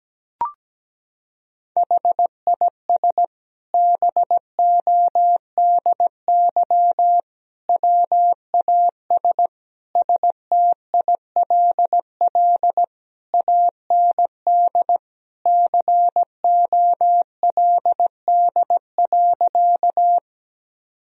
Morse code, heard as a keyed tone, about 17 wpm, 700 Hz. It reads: HIS BODY WAS STILL AND COLD.
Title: Sentences from Top 1000 Words - Review Part 2 - 17wpm